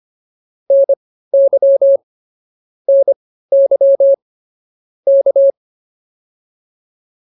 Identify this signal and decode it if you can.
Morse code, heard as a keyed tone, about 25 wpm, spaced as for 15 wpm, 550 Hz. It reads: NY NY K